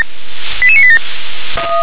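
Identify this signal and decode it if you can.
Five-tone selective call (ZVEI), heard as quick runs of five tones, tones of about 80 ms. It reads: 88987
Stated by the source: ZVEI five-tone call received with an SDR.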